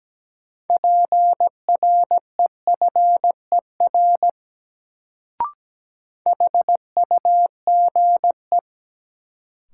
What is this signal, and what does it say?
Morse code, heard as a keyed tone, about 17 wpm, 700 Hz. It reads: PREFER HUGE
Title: Sentences from Top 1100 Words - New Word Drill - 17wpm